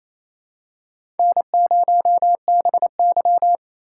Morse code, heard as a keyed tone, about 27 wpm, 700 Hz. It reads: N0BY
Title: USA Call Signs - 4 Characters - 2x Character Spacing - 28wpm